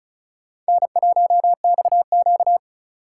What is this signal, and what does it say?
Morse code, heard as a keyed tone, about 34 wpm, 690 Hz. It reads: N1XQ
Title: USA Call Signs - 4 Characters - 2x Character Spacing - 35wpm